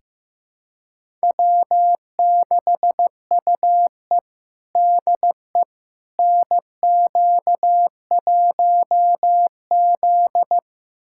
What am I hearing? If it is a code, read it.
Morse code, heard as a keyed tone, about 16 wpm, 700 Hz. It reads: W6UE DE NQ1Z
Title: USA Callsign DE Callsign - 4 Characters - 15wpm